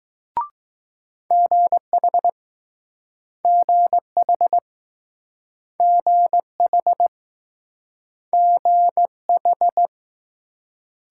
Morse code, heard as a keyed tone, about 20 wpm, 700 Hz. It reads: GH GH GH GH